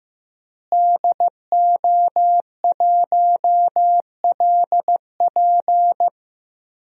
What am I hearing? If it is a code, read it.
Morse code, heard as a keyed tone, about 15 wpm, 700 Hz. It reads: DO1LP
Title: DXCC-100 Call Signs - 5 Characters - Repeated 2x - 15wpm